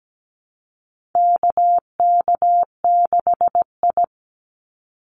Morse code, heard as a keyed tone, about 17 wpm, 700 Hz. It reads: KK6I
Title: Call Signs Repeated 2x - 17wpm